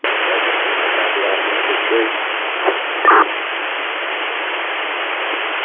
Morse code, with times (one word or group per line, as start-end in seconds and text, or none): none